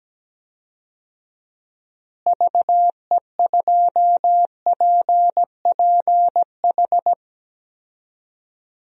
Morse code, heard as a keyed tone, about 17 wpm, 700 Hz. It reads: VE2PPH
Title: DXCC-100 Call Signs - 6 Characters - Repeated 3x - 17wpm